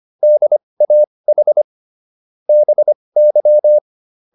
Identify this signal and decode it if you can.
Morse code, heard as a keyed tone, about 25 wpm, spaced as for 20 wpm, 600 Hz. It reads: DAH BY
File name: SS315 20